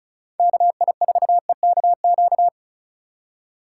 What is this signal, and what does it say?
Morse code, heard as a keyed tone, about 36 wpm, 700 Hz. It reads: KI4EKQ
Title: USA Call Signs - 6 Characters - 2x Character Spacing - 35wpm